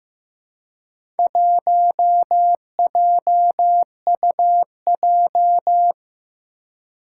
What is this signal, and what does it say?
Morse code, heard as a keyed tone, about 15 wpm, 700 Hz. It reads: 1JUJ
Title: Four Challenging Letter-Number Combinations 15wpm